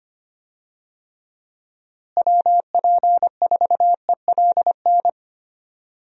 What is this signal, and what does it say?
Morse code, heard as a keyed tone, about 25 wpm, 700 Hz. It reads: WP4ELN
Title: USA Call Signs - 6 Characters - Repeated 3x - 25wpm